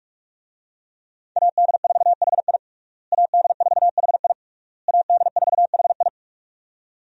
Morse code, heard as a keyed tone, about 42 wpm, 700 Hz. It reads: AD4HI AD4HI AD4HI
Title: USA Call Signs - 5 Characters - Repeated 3x - 45wpm